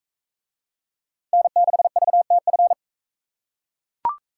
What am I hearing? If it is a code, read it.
Morse code, heard as a keyed tone, about 39 wpm, 700 Hz. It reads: N6VTF